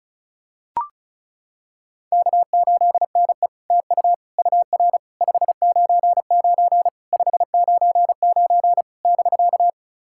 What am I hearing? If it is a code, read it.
Morse code, heard as a keyed tone, about 36 wpm, 700 Hz. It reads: K8NE TU UR 599 599 <BK>
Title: SOTA QSO Practice 35wpm